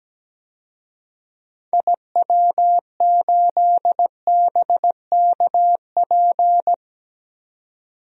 Morse code, heard as a keyed tone, about 17 wpm, 700 Hz. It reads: IW8BKP
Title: DXCC-100 Call Signs - 6 Characters - 17wpm